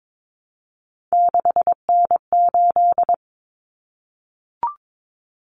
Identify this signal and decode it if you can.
Morse code, heard as a keyed tone, about 22 wpm, 700 Hz. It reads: 6N8
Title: Common Cut Numbers - 3-Digit 22wpm